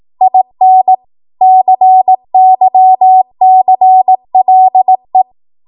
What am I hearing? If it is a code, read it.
Morse code, heard as a keyed tone, about 18 wpm, 750 Hz. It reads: IN CYCLE
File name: Archive / 18 WPM / 18WPM